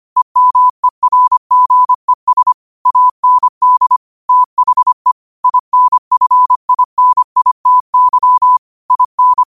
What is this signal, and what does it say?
Morse code, heard as a keyed tone, about 25 wpm, 1000 Hz. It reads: EMERGES AND THE INFINITY IN
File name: qotdfiles / 25wpm